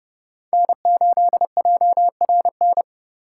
Morse code, heard as a keyed tone, about 32 wpm, 700 Hz. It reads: N8JRN
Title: USA Call Signs - 5 Characters - 4x Character Spacing - 30wpm